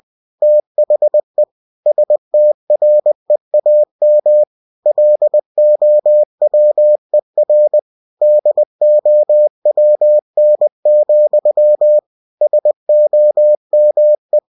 Morse code, heard as a keyed tone, about 20 wpm, 600 Hz. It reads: THE STREAM LOWER DOWN, SOME
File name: winnie-20wpm